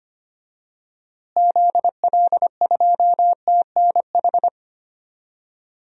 Morse code, heard as a keyed tone, about 25 wpm, 700 Hz. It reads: ZL2TNH
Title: DXCC-100 Call Signs - 6 Characters - Repeated 2x - 25wpm